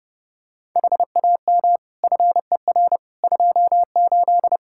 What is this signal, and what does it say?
Morse code, heard as a keyed tone, about 30 wpm, 700 Hz. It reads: HAM FER 28
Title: Practice QSO - Exchange 3 - More Info - 30wpm